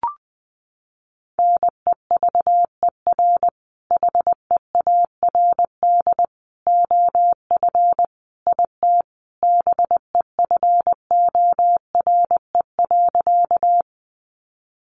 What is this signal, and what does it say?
Morse code, heard as a keyed tone, about 20 wpm, 700 Hz. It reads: NEVER HEARD OF IT BEFORE.
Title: Sentences from Top 400 Words 20wpm